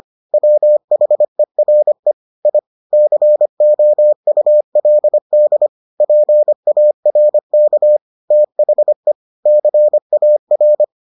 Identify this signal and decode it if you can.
Morse code, heard as a keyed tone, about 25 wpm, 600 Hz. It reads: WHERE I COULD PARK THE CAR